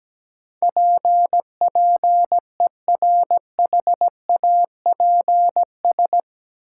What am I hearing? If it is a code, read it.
Morse code, heard as a keyed tone, about 17 wpm, 700 Hz. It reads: PPERHAPS
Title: Sentences from Top 1100 Words - New Word Drill Alt - 17wpm